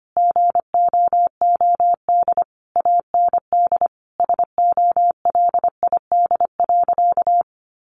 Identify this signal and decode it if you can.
Morse code, heard as a keyed tone, about 25 wpm, 700 Hz. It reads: GOOD AND SOLID.
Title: Sentences from Top 1500 Words 25wpm